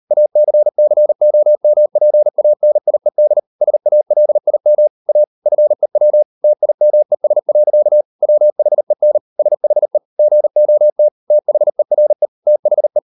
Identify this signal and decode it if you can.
Morse code, heard as a keyed tone, about 38 wpm, 600 Hz. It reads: ACCOMPANIED SALIM A FEW TIMES. WHEN SHE GOT THERE THE